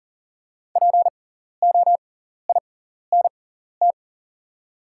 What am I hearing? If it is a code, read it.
Morse code, heard as a keyed tone, about 40 wpm, spaced as for 14 wpm, 700 Hz. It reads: POINT
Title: 6x Character Spacing - Top 300 Words - 40wpm